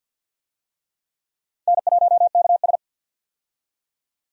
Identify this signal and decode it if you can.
Morse code, heard as a keyed tone, about 47 wpm, 700 Hz. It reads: N1KS
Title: USA Call Signs - 4 Characters - 50wpm